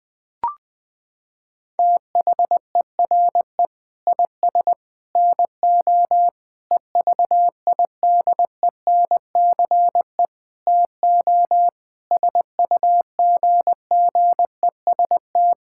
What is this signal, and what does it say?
Morse code, heard as a keyed tone, about 20 wpm, 700 Hz. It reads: THERE IS NO EVIDENCE TO SUGGEST
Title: Sentences with 7-Word Phrases - 20wpm